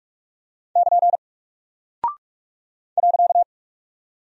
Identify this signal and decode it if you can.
Morse code, heard as a keyed tone, about 44 wpm, 700 Hz. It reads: <KN> .
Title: Single Character (Letter:Number:Punctuation:Prosign) Alt 45wpm